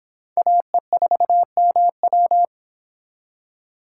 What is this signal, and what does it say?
Morse code, heard as a keyed tone, about 25 wpm, 700 Hz. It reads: AE4MW